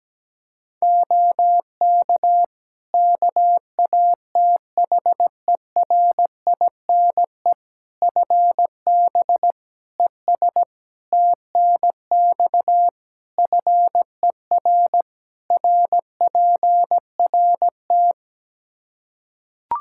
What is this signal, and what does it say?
Morse code, heard as a keyed tone, about 17 wpm, 700 Hz. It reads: OK KATHERINE FB ES TNX FER RPRT
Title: K1USN SST QSO Practice 17wpm